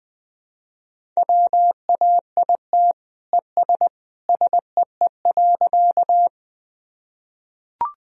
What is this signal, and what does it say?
Morse code, heard as a keyed tone, about 20 wpm, 700 Hz. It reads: WAIT ES SEE.
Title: Sentences from Top 500 Words and Abbreviations 20wpm